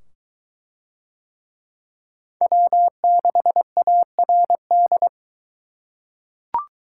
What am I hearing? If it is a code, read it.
Morse code, heard as a keyed tone, about 20 wpm, 700 Hz. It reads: W6ARD